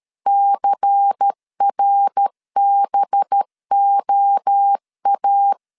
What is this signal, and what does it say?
Morse code, heard as a keyed tone, about 13 wpm, 800 Hz. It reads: CRBOA
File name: LEZIONI / 6 PKX0 / 6 PK 12wpm 2006